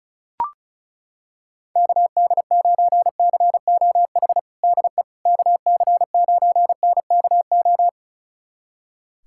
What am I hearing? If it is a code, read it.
Morse code, heard as a keyed tone, about 36 wpm, 700 Hz. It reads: KD9COH DE KC9NKO
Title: USA Callsign DE Callsign - 6 Characters - 35wpm